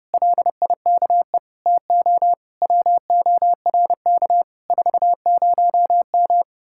Morse code, heard as a keyed tone, about 33 wpm, 700 Hz. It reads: LIKE TO WORK 40M